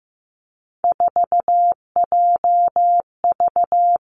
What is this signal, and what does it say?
Morse code, heard as a keyed tone, about 15 wpm, 700 Hz. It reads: 4JV
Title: Three Challenging Letter-Number Combinations 15wpm